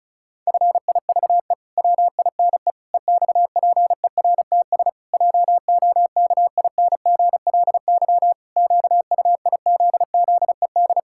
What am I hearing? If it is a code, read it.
Morse code, heard as a keyed tone, about 34 wpm, 700 Hz. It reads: FIVE WINE EXPERTS JOKINGLY QUIZZED
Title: Pangram Challenge v2 - 35wpm